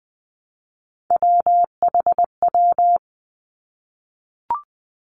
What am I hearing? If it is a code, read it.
Morse code, heard as a keyed tone, about 20 wpm, 700 Hz. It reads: WHW
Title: Three Challenging Letter-Number Combinations 20wpm